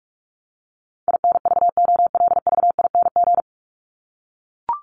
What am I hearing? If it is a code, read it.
Morse code, heard as a keyed tone, about 44 wpm, 700 Hz. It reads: INVOLVING
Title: ING Suffix - Top 100 Words with Base Words 45wpm